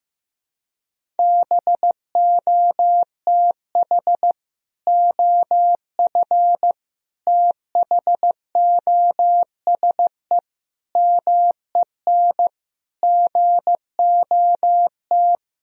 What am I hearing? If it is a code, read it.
Morse code, heard as a keyed tone, about 15 wpm, 700 Hz. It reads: BOTH OF THOSE MEN GOT
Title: Sentences from Top 1000 Words - Review Part 1 - 15wpm